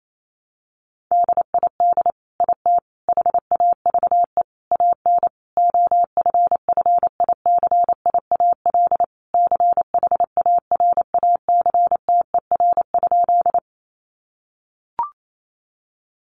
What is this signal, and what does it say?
Morse code, heard as a keyed tone, about 28 wpm, 700 Hz. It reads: DID IT HAVE AN OFFICIAL CHARACTER?
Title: Sentences from Top 1200 Words 28wpm